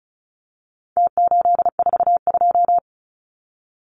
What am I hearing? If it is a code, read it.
Morse code, heard as a keyed tone, about 34 wpm, 700 Hz. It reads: T842
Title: Numbers - Contest Style - 4-Digit 35wpm